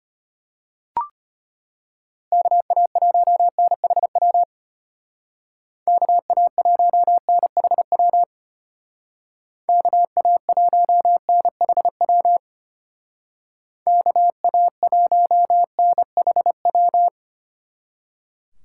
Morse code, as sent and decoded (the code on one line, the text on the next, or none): -.- .- .---- -. .... .-- / -.- .- .---- -. .... .-- / -.- .- .---- -. .... .-- / -.- .- .---- -. .... .--
KA1NHW KA1NHW KA1NHW KA1NHW